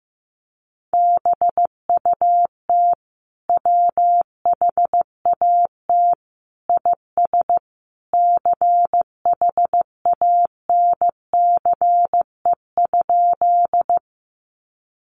Morse code, heard as a keyed tone, about 14 wpm, 700 Hz. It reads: BUT WHAT IS CHANCE?
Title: Sentences from Top 800 Words 15wpm